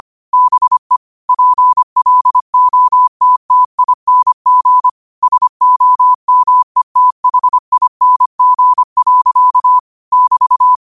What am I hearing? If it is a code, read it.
Morse code, heard as a keyed tone, about 25 wpm, 1000 Hz. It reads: BE PLOTTING SOMETHING. <BT>